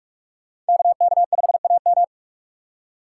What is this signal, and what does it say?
Morse code, heard as a keyed tone, about 44 wpm, 700 Hz. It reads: KK5AM